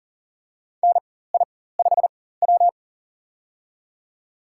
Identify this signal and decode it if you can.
Morse code, heard as a keyed tone, about 40 wpm, spaced as for 19 wpm, 700 Hz. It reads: NI5W
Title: USA Call Signs - 4 Characters - 4x Character Spacing - 40wpm